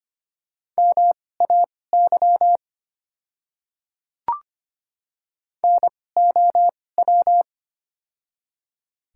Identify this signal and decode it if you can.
Morse code, heard as a keyed tone, about 25 wpm, spaced as for 18 wpm, 690 Hz. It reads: MAY NOW
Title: 2x Character Spacing - Top 200 Words - 25wpm